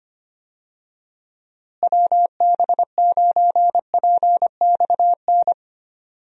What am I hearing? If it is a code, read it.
Morse code, heard as a keyed tone, about 25 wpm, 700 Hz. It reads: WB9PXN
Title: USA Call Signs - 6 Characters - Repeated 3x - 25wpm